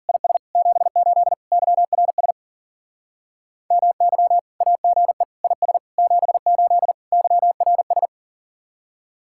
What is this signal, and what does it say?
Morse code, heard as a keyed tone, about 42 wpm, 700 Hz. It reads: IS 78 YRS MY AGE IS 78 YRS
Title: Speed-Racing Common QSO Phrases - 40wpm